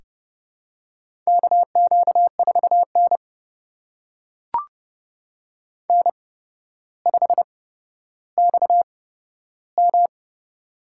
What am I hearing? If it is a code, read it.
Morse code, heard as a keyed tone, about 30 wpm, spaced as for 13 wpm, 700 Hz. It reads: KQ4N N5XM
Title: USA Call Signs - 4 Characters - 8x Character Spacing - 30wpm